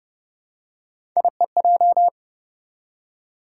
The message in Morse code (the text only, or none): IEJ